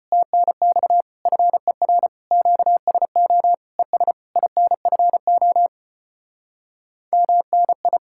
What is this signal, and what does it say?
Morse code, heard as a keyed tone, about 33 wpm, 700 Hz. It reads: TNX FER QSO ES INFO MNI